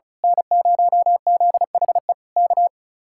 Morse code, heard as a keyed tone, about 33 wpm, 700 Hz. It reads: N0ZHE K
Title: Speed-Racing Common QSO Phrases - 35wpm